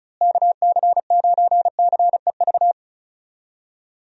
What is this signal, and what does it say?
Morse code, heard as a keyed tone, about 36 wpm, 700 Hz. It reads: KC9CEV
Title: USA Call Signs - 6 Characters - Repeated 2x - 35wpm